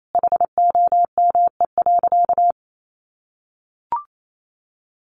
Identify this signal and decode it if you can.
Morse code, heard as a keyed tone, about 28 wpm, 700 Hz. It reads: HOME.
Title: Sentences from Top 200 Words 28wpm